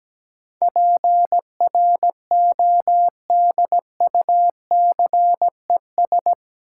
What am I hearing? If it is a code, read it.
Morse code, heard as a keyed tone, about 17 wpm, 700 Hz. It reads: PRODUCES
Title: Sentences from Top 1000 Words - Review Part 6 - New Word Drill - 17wpm